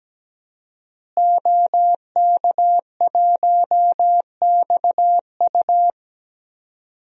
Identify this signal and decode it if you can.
Morse code, heard as a keyed tone, about 17 wpm, 700 Hz. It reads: OK1XU
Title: DXCC-100 Call Signs - 5 Characters - Repeated 2x - 17wpm